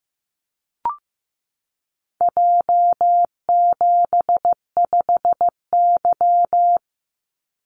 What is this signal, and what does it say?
Morse code, heard as a keyed tone, about 16 wpm, 700 Hz. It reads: J75Y
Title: Call Signs Repeated 3x - 15wpm